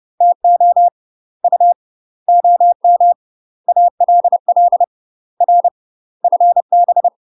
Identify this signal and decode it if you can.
Morse code, heard as a keyed tone, about 30 wpm, 700 Hz. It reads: TO U OM ALL R FB
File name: UK MASTER QSOs TRAINING / 30WPM / UK MASTER QSOs 0009